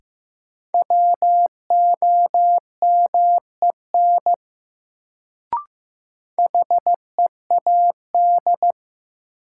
Morse code, heard as a keyed tone, about 15 wpm, 700 Hz. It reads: WOMEN HEAD
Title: Sets of 3 Words - New Word Drill - 15wpm